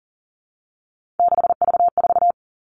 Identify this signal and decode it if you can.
Morse code, heard as a keyed tone, about 40 wpm, 700 Hz. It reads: 6V4
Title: All Cut Numbers - 3-Digit 40wpm